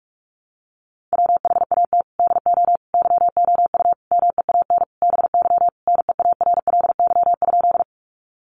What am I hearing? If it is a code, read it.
Morse code, heard as a keyed tone, about 44 wpm, 700 Hz. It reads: WHAT DO YOU MEAN BY NEARLY?
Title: Sentences from Top 1300 Words 45wpm